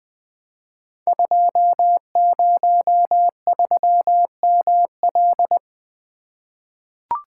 Morse code, heard as a keyed tone, about 20 wpm, 700 Hz. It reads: 203ML
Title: Code Groups - 5 Letter:Numbers - 20wpm